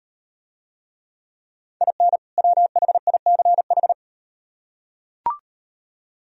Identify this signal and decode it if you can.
Morse code, heard as a keyed tone, about 38 wpm, 700 Hz. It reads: IN WHICH